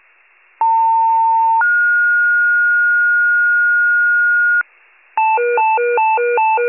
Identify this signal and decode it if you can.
Two-tone page, heard as a one-second tone, then a three-second tone, about 900 then 1430 Hz, with noise. It is heard from 0.6 to 4.6 s.
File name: wcalert